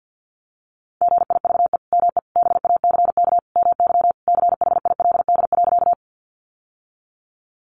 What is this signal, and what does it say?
Morse code, heard as a keyed tone, about 50 wpm, 700 Hz. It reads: GIVE ME BACK MY CHILD.
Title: Sentences from Top 800 Words 50wpm